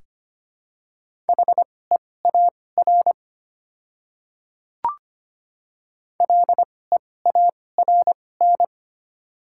Morse code, heard as a keyed tone, about 25 wpm, spaced as for 18 wpm, 700 Hz. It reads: HEAR LEARN
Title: 2x Character Spacing - Top 300 Words - 25wpm